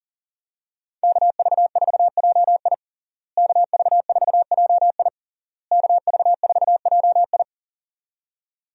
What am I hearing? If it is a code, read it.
Morse code, heard as a keyed tone, about 40 wpm, 700 Hz. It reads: KV4JI KV4JI KV4JI
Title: USA Call Signs - 5 Characters - Repeated 3x - 40wpm